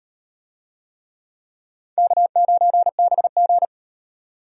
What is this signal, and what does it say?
Morse code, heard as a keyed tone, about 36 wpm, 700 Hz. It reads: K9BG